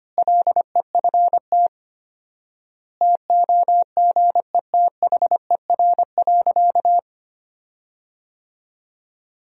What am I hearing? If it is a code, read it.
Morse code, heard as a keyed tone, about 25 wpm, 700 Hz. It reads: LEFT TOGETHER.
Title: 4x Word-Spacing - Sentences from Top 400 Words - 25wpm